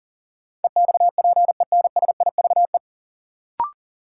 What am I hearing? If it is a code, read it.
Morse code, heard as a keyed tone, about 40 wpm, 700 Hz. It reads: EXPENSIVE
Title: Sentences from Top 1000 Words - Review Part 6 - New Word Drill - 40wpm